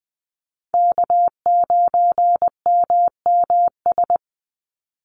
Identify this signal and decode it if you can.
Morse code, heard as a keyed tone, about 20 wpm, 700 Hz. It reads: K9MMS
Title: Call Signs Repeated 2x - 20wpm